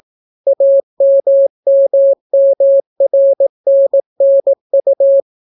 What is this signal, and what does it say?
Morse code, heard as a keyed tone, about 18 wpm, 550 Hz. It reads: AMMMRNNU